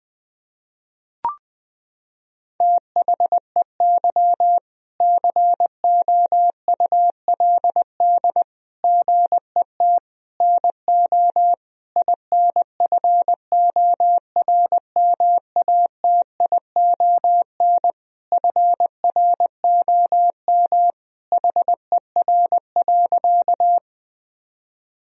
Morse code, heard as a keyed tone, about 20 wpm, 700 Hz. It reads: THEY COULD GET NO INFORMATION FROM HER.
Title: Sentences from Top 1000 Words - Review Part 3 - 20wpm